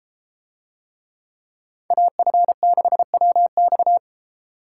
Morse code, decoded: AF6WX

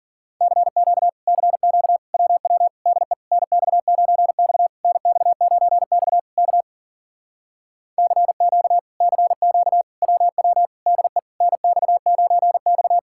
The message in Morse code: CQ CQ WW DE NX9X NX9X K CQ CQ WW DE NX9X

